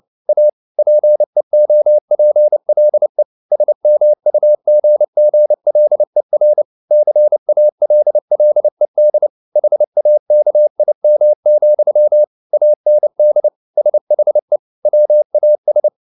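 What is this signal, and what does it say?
Morse code, heard as a keyed tone, about 29 wpm, 600 Hz. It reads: A PEOPLE SMUGGLER CALLED HAKIM, AND SHE WAS
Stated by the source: Morse code at 29 wpm, sending Ken Follett's novel Never in English, from CW med Gnister.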